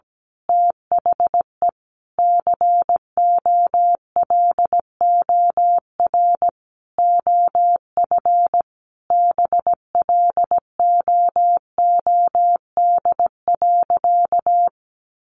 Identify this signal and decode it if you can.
Morse code, heard as a keyed tone, about 17 wpm, 700 Hz. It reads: THE COLOR OF BLOOD.
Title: A Sound Of Thunder 17wpm